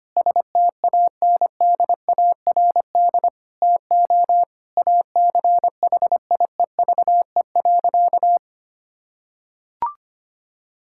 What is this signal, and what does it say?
Morse code, heard as a keyed tone, about 25 wpm, 700 Hz. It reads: STANDARD TO ACHIEVE.
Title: Sentences from Top 1000 Words - Review Part 6 - 25wpm